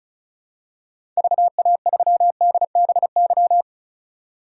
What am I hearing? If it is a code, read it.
Morse code, heard as a keyed tone, about 34 wpm, 700 Hz. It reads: VA3DBY
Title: DXCC-100 Call Signs - 6 Characters - 35wpm